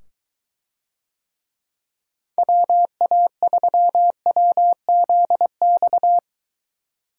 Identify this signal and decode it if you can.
Morse code, heard as a keyed tone, about 23 wpm, 700 Hz. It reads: WA3WZX